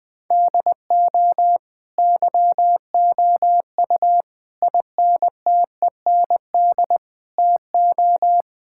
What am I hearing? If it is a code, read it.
Morse code, heard as a keyed tone, about 20 wpm, 700 Hz. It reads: DO YOU INTEND TO
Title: Sentences from Top 1600 Words 20wpm